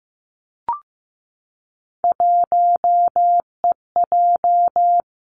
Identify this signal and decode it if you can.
Morse code, heard as a keyed tone, about 15 wpm, 700 Hz. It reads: 1EJ
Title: Three Challenging Letter-Number Combinations 15wpm